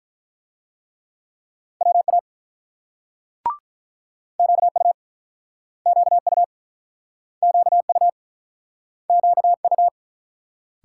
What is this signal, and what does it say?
Morse code, heard as a keyed tone, about 47 wpm, 700 Hz. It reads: WA QU QU QU QU